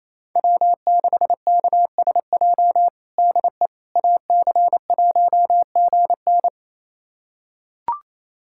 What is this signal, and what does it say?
Morse code, heard as a keyed tone, about 28 wpm, 700 Hz. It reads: W6KSJ DE AC1GN
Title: USA Callsign DE Callsign - 5 Characters - 28wpm